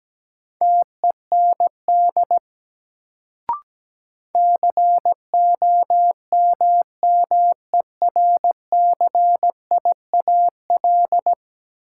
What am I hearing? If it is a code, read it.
Morse code, heard as a keyed tone, about 17 wpm, 700 Hz. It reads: TEND COMMERCIAL
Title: Sentences from Top 1000 Words - Review Part 5 - New Word Drill - 17wpm